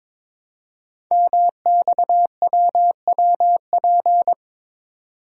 Morse code, heard as a keyed tone, about 22 wpm, 700 Hz. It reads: MXWWP
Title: Code Groups - 5 Letters - 22wpm